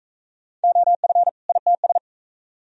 Morse code, heard as a keyed tone, about 43 wpm, 700 Hz. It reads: OF ITS